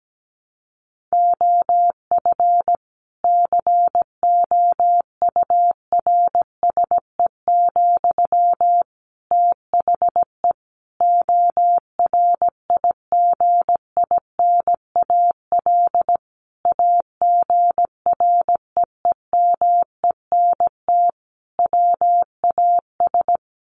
Morse code, heard as a keyed tone, about 17 wpm, 700 Hz. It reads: OF COURSE, THE ORIGINAL AGREEMENT WAS
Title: Sentences from Top 1500 Words 17wpm